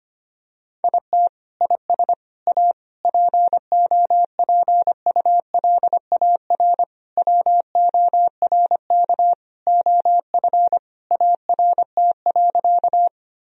Morse code, heard as a keyed tone, about 25 wpm, 700 Hz. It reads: IT IS A POPULAR WORK OF ART.
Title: Sentences from Top 1000 Words - Review Part 3 - 25wpm